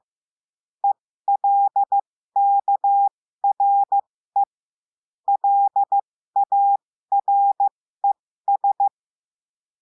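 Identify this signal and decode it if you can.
Morse code, heard as a keyed tone, about 15 wpm, spaced as for 13 wpm, 800 Hz. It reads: ELKRE LARES
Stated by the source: Morse lesson adding the letter L, five-letter random groups.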